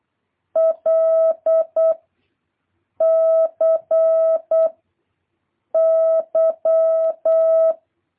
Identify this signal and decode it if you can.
Morse code, heard as a keyed tone, about 8 wpm, 650 Hz. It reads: L C Y
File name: jh8cbh / 2023cw